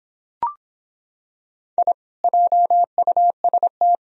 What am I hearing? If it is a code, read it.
Morse code, heard as a keyed tone, about 26 wpm, 700 Hz. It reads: I JUST